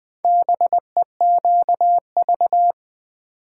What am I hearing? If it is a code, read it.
Morse code, heard as a keyed tone, about 20 wpm, 700 Hz. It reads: BEQV
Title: Four Letter-Number Combinations - 20wpm